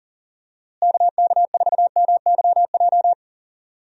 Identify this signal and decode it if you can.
Morse code, heard as a keyed tone, about 40 wpm, 700 Hz. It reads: KK4MYJ
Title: USA Call Signs - 6 Characters - Repeated 2x - 40wpm